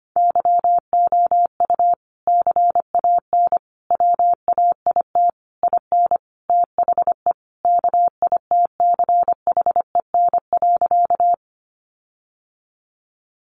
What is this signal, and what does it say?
Morse code, heard as a keyed tone, about 25 wpm, 700 Hz. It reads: YOU CAN WAIT IN THE KITCHEN.
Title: Sentences from Top 1400 Words 25wpm